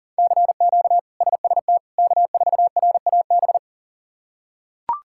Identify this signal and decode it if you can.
Morse code, heard as a keyed tone, about 40 wpm, 700 Hz. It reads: CQ SST K4RAB
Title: K1USN SST QSO Practice - 40wpm